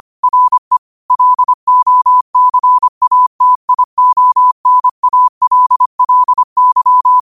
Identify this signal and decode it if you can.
Morse code, heard as a keyed tone, about 25 wpm, 1000 Hz. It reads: RE LOCATIONALLY
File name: qotdfiles / 25wpm